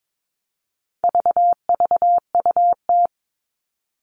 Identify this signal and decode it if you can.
Morse code, heard as a keyed tone, about 22 wpm, 700 Hz. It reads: VVUT
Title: All Cut Numbers - 4-Digit 22wpm